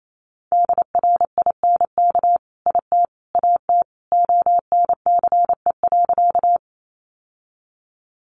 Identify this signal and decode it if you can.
Morse code, heard as a keyed tone, about 28 wpm, 700 Hz. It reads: DRINK IT AT ONCE.
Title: Sentences from Top 1500 Words 28wpm